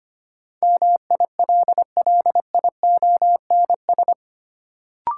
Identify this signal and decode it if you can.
Morse code, heard as a keyed tone, about 25 wpm, 700 Hz. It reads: MILLIONS
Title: Sets of 5 Words - New Word Drill - 25wpm